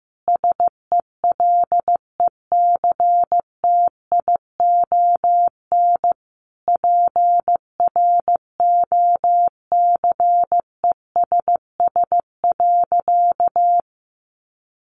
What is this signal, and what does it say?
Morse code, heard as a keyed tone, about 15 wpm, 700 Hz. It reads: SELECTION PROCESS.
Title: Sentences from Top 1400 Words 15wpm